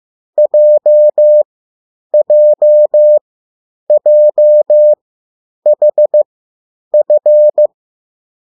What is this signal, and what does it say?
Morse code, heard as a keyed tone, about 15 wpm, spaced as for 9 wpm, 600 Hz. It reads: JJJHF
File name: urok 1